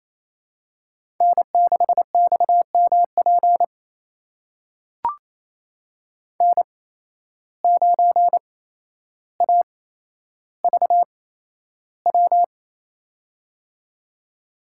Morse code, heard as a keyed tone, about 27 wpm, spaced as for 12 wpm, 700 Hz. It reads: N6XMP N9AVW
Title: USA Call Signs - 5 Characters - 8x Character Spacing - 28wpm